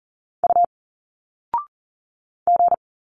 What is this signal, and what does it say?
Morse code, heard as a keyed tone, about 40 wpm, 700 Hz. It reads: U G
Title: Single Letters 40wpm